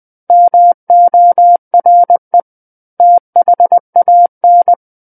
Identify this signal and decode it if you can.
Morse code, heard as a keyed tone, about 20 wpm, 700 Hz. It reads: MORE THAN
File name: 20 WPM